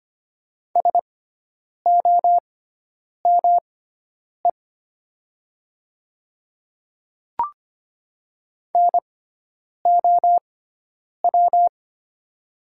Morse code, heard as a keyed tone, about 25 wpm, spaced as for 9 wpm, 700 Hz. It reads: SOME NOW
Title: 6x Character Spacing - Top 100 Words - 25wpm